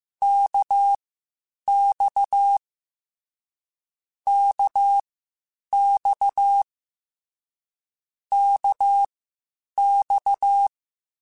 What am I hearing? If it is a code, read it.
Morse code, heard as a keyed tone, about 15 wpm, spaced as for 8 wpm, 800 Hz. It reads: KX KX KX
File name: Short Runs / Short Runs-mp3s / Balanced @13